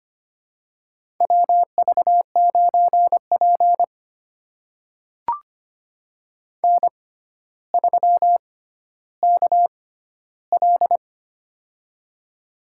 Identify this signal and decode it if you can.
Morse code, heard as a keyed tone, about 25 wpm, spaced as for 13 wpm, 700 Hz. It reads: WV9P N3KL
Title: USA Call Signs - 4 Characters - 6x Character Spacing - 25wpm